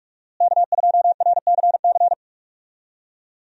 Morse code, heard as a keyed tone, about 41 wpm, 700 Hz. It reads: K2RCC
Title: USA Call Signs - 5 Characters - Repeated 2x - 45wpm